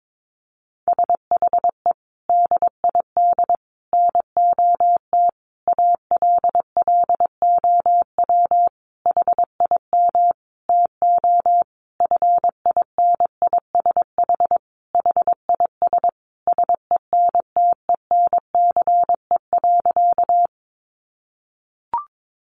Morse code, heard as a keyed tone, about 22 wpm, 700 Hz. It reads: SHE DID NOT ALLOW HIM TO FINISH HIS SENTENCE.
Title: Sentences from Top 1200 Words 22wpm